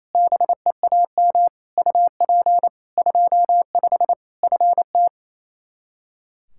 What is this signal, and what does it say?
Morse code, heard as a keyed tone, about 28 wpm, 700 Hz. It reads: BEAM UP 25 FT
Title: Practice QSO - Exchange 2 - Info - 28wpm